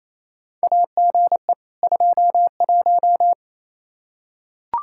Morse code, heard as a keyed tone, about 28 wpm, 700 Hz. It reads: AGE 21
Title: Practice QSO - Exchange 3 - More Info - 28wpm